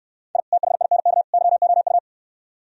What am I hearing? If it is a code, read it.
Morse code, heard as a keyed tone, about 68 wpm, 700 Hz. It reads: I THINK YOU